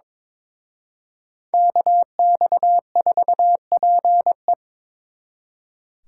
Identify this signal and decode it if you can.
Morse code, heard as a keyed tone, about 25 wpm, 700 Hz. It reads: KX4PE
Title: Speed Racing - USA Call Signs 22wpm